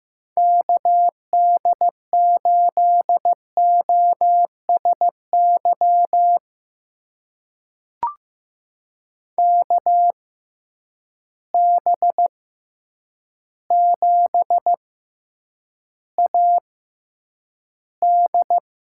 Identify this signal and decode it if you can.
Morse code, heard as a keyed tone, about 15 wpm, spaced as for 8 wpm, 700 Hz. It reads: KD8OSY KB7AD
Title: USA Call Signs - 6 Characters - 6x Character Spacing - 15wpm